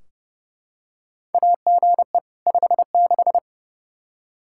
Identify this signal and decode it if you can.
Morse code, heard as a keyed tone, about 30 wpm, 700 Hz. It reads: AGE 56
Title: K1USN SST QSO Practice 30wpm